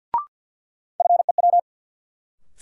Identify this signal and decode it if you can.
Morse code, heard as a keyed tone, about 52 wpm, 700 Hz. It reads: FEW
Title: Sets of 2 Words - New Word Drill - 50wpm